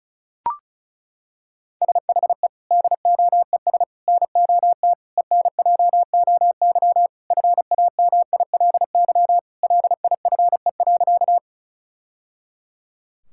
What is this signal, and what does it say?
Morse code, heard as a keyed tone, about 36 wpm, 700 Hz. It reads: SHE DOES NOT ENJOY FAMILY LIFE.
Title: Sentences from Top 1000 Words - Review Part 5 - 35wpm